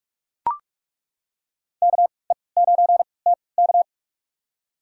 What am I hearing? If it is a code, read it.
Morse code, heard as a keyed tone, about 42 wpm, spaced as for 25 wpm, 700 Hz. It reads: KE9TK
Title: USA Call Signs - 5 Characters - 3x Character Spacing - 45wpm